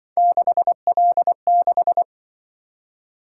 Morse code, heard as a keyed tone, about 25 wpm, 700 Hz. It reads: 6L6
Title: Speed-Racing - Three Letter-Number Combinations - 20wpm